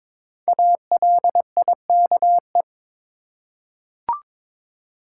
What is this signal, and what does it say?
Morse code, heard as a keyed tone, about 22 wpm, 700 Hz. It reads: ALIKE
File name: 5 Letter Words - First 3 Letters Given - 22wpm